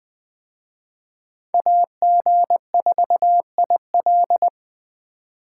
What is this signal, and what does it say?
Morse code, heard as a keyed tone, about 20 wpm, 700 Hz. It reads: AG4IL
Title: Speed Racing - USA 2x2 Call Signs 17wpm